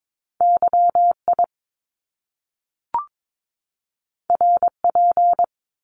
Morse code, heard as a keyed tone, about 22 wpm, 700 Hz. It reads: YI RP
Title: Top Two Letter Combinations in Words 22wpm